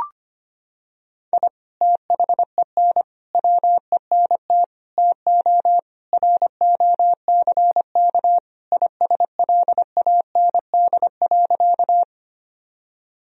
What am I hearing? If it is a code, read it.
Morse code, heard as a keyed tone, about 25 wpm, 700 Hz. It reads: I THEN WENT TO ROCK ISLAND.
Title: Sentences from Top 1000 Words - Review Part 3 - 25wpm